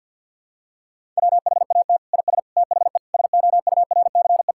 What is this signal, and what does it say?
Morse code, heard as a keyed tone, about 52 wpm, 700 Hz. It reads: WHAT IS THE SOURCE
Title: Sentences from Top 1000 Words - Review Part 5 - 50wpm